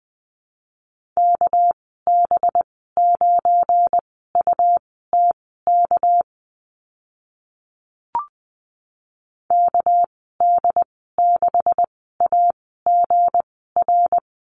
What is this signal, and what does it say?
Morse code, heard as a keyed tone, about 20 wpm, spaced as for 14 wpm, 700 Hz. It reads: KB9UTK KD6AGR
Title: 213 Lesson 34g - Number 8 - Call Signs 20wpm